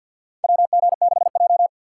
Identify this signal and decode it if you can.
Morse code, heard as a keyed tone, about 53 wpm, 700 Hz. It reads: WG6J
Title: USA Call Signs - 4 Characters - 3x Character Spacing - 50wpm